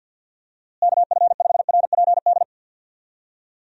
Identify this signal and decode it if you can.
Morse code, heard as a keyed tone, about 50 wpm, 700 Hz. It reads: KF5RPD
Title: USA Call Signs - 6 Characters - Repeated 3x - 50wpm